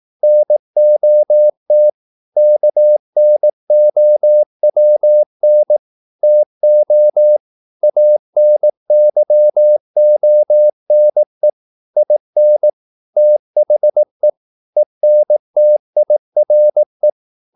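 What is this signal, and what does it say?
Morse code, heard as a keyed tone, about 18 wpm, 600 Hz. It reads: NOT KNOWN TO ANYONE IN THE ENTIRE
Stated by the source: Morse practice at 18 wpm, English text, from CW med Gnister.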